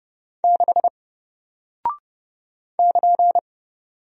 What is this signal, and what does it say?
Morse code, heard as a keyed tone, about 30 wpm, 700 Hz. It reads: 6 <KN>
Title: Single Character (Letter:Number:Punctuation:Prosign) Alt 30wpm